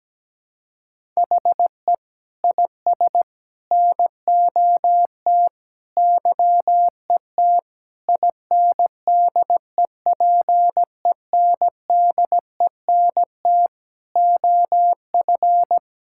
Morse code, heard as a keyed tone, about 17 wpm, 700 Hz. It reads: HE IS NOT YET INDEPENDENT OF
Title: Sentences from Top 1000 Words - Review Part 6 - 17wpm